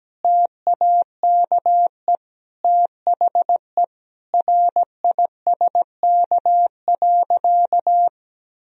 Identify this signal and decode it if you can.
Morse code, heard as a keyed tone, about 17 wpm, 700 Hz. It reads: TAKE THE RISK.
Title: Sentences from Top 1000 Words - Review Part 4 - 17wpm